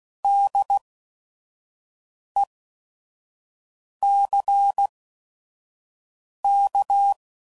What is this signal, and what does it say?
Morse code, heard as a keyed tone, about 16 wpm, spaced as for 5 wpm, 800 Hz. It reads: DECK
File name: Prefixes / By First Letter / Ds